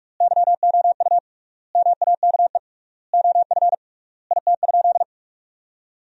Morse code, heard as a keyed tone, about 46 wpm, 700 Hz. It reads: YOU MAKE OF IT?
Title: 3x Word-Spacing - Sentences from Top 100 Words - 45wpm